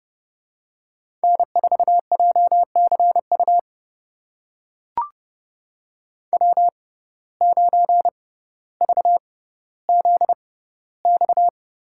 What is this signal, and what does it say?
Morse code, heard as a keyed tone, about 30 wpm, spaced as for 15 wpm, 700 Hz. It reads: N4JCU W9VZX
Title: USA Call Signs - 5 Characters - 6x Character Spacing - 30wpm